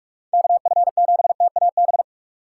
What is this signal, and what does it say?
Morse code, heard as a keyed tone, about 46 wpm, 700 Hz. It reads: KF7TAB
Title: USA Call Signs - 6 Characters - 4x Character Spacing - 45wpm